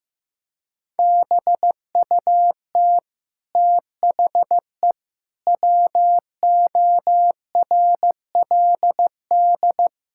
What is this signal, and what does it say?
Morse code, heard as a keyed tone, about 15 wpm, 700 Hz. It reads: BUT THE WORLD
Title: Sentences from Top 1000 Words - Review Part 6 - 15wpm